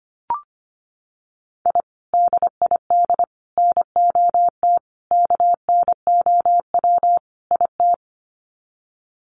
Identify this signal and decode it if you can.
Morse code, heard as a keyed tone, about 25 wpm, 700 Hz. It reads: I DID NOT KNOW IT
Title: Sets of 5 Words Encore2 25wpm